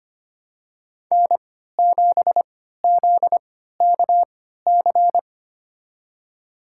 Morse code, heard as a keyed tone, about 25 wpm, spaced as for 14 wpm, 700 Hz. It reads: N7ZKC
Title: USA Call Signs - 5 Characters - 3x Character Spacing - 25wpm